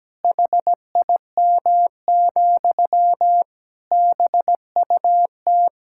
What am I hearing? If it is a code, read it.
Morse code, heard as a keyed tone, about 17 wpm, 700 Hz. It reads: HIM, BUT
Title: Sentences from Top 1000 Words - Review Part 5 - 17wpm